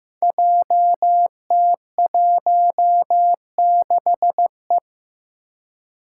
Morse code, heard as a keyed tone, about 15 wpm, 700 Hz. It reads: JT16E
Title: Five Challenging Letter-Number Combinations 15wpm